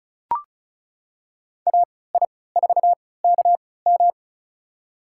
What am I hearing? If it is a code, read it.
Morse code, heard as a keyed tone, about 34 wpm, spaced as for 20 wpm, 700 Hz. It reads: AI4KM
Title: USA Call Signs - 5 Characters - 3x Character Spacing - 35wpm